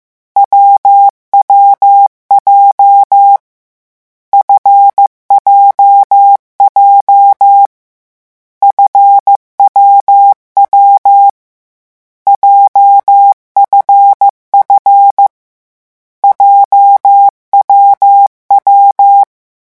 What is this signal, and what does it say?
Morse code, heard as a keyed tone, about 15 wpm, 800 Hz. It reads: WWJ FJJ FWW JFF JWW
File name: morse mp3 / lecon7-vitesse 14